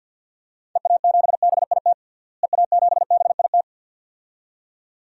Morse code, heard as a keyed tone, about 47 wpm, 700 Hz. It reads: EA7BIT EA7BIT
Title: DXCC-100 Call Signs - 6 Characters - Repeated 2x - 50wpm